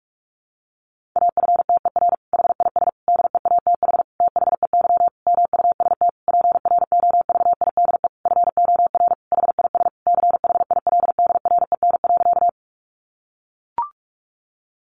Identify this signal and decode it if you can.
Morse code, heard as a keyed tone, about 44 wpm, 700 Hz. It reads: AFTER HIS DEATH THEY MUST PROVIDE FOR HIS CHILDREN.
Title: Sentences from Top 1100 Words 45wpm